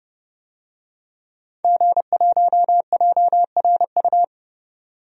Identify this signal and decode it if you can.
Morse code, heard as a keyed tone, about 30 wpm, 700 Hz. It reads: G1JRU
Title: DXCC-100 Call Signs - 5 Characters - 30wpm